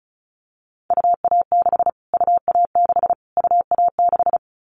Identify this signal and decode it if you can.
Morse code, heard as a keyed tone, about 34 wpm, 700 Hz. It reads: UA6 UA6 UA6
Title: Callsign Prefixes for Top 25 DXCC - Repeated 3x - 35wpm